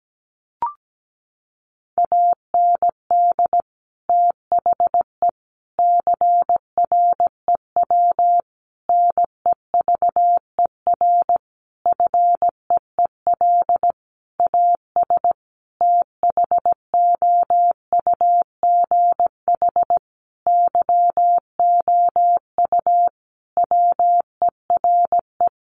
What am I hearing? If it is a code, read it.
Morse code, heard as a keyed tone, about 17 wpm, 700 Hz. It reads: AND THE CREW NEVER FEEL AS THOUGH YOU WERE
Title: Sentences from Top 1500 Words 17wpm